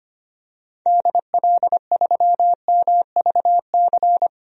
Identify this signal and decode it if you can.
Morse code, heard as a keyed tone, about 25 wpm, 700 Hz. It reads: DL3MVC
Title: DXCC-100 Call Signs - 6 Characters - Repeated 3x - 25wpm